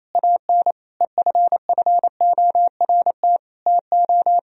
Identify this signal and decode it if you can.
Morse code, heard as a keyed tone, about 33 wpm, 700 Hz. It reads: AN EFFORT TO
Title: Speed-Racing - Sets of 4 Words 28wpm